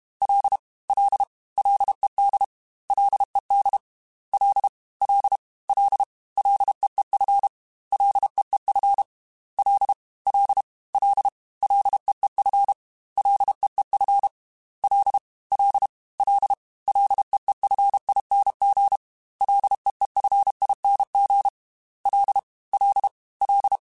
Morse code, heard as a keyed tone, about 32 wpm, 800 Hz. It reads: L L LED LED L L L LEEF LEEF L L L LEEF LEEF L L L LEEFING LEEFING L L L
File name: Ls / Single L m3s / Single L Prefixe 3x2 Twice @30